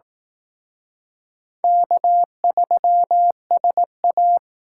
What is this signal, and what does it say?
Morse code, heard as a keyed tone, about 18 wpm, 700 Hz. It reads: K3SA
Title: Speed Racing - USA 1x2 Call Signs 15wpm